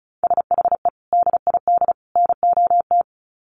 Morse code, heard as a keyed tone, about 34 wpm, 700 Hz. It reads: SHE DID NOT
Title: Sets of 3 Words 35wpm